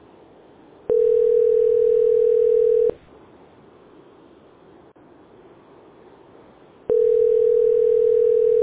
Ringback tone (North America), 440 + 480 Hz.